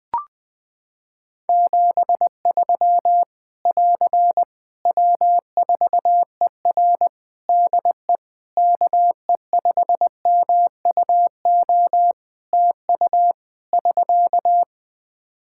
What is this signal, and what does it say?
Morse code, heard as a keyed tone, about 21 wpm, 700 Hz. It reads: 73 <AR> W4ER DE KE5MUO TU <SK>
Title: K1USN SST QSO Practice 20wpm